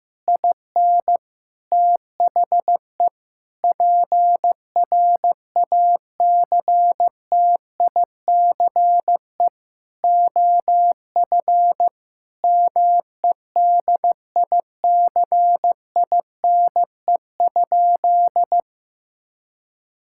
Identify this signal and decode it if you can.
Morse code, heard as a keyed tone, about 15 wpm, 700 Hz. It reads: IN THE PRACTICE OF MEDICINE?
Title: Sentences from Top 1000 Words - Review Part 6 - 15wpm